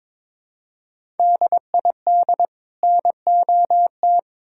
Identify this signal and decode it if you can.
Morse code, heard as a keyed tone, about 22 wpm, 700 Hz. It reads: DID NOT
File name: practice / 22wpm / Build A Phrase - 5-Word Phrases - 22wpm